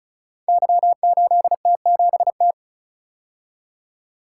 Morse code, heard as a keyed tone, about 36 wpm, 690 Hz. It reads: Y8T7T
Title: Code Groups - 5 Letter:Numbers - 35wpm